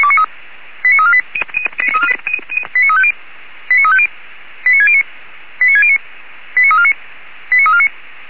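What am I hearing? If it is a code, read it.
Five-tone selective call (ZVEI), heard as quick runs of five tones, tones of about 70 ms. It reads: 89348 89350 89350 89609 89609 89349 89349
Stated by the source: ZVEI five-tone selective call received on an SDR.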